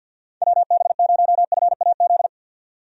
Morse code, heard as a keyed tone, about 52 wpm, 700 Hz. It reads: WB0FAZ
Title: USA Call Signs - 6 Characters - 50wpm